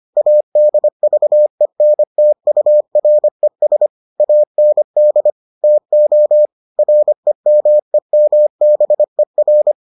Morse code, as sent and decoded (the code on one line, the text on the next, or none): .- -.. ...- . -. - ..- .-. ... / .- -. -.. / - --- / .-. -- . -- -... .-.
ADVENTURES AND TO REMEMBER